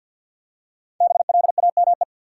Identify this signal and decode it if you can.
Morse code, heard as a keyed tone, about 50 wpm, 700 Hz. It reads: BLAME
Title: Five Letter Words - 50wpm